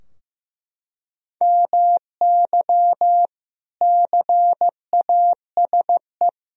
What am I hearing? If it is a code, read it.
Morse code, heard as a keyed tone, about 15 wpm, 700 Hz. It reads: MY CASE
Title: Sentences from Top 1000 Words - Review Part 2 - 15wpm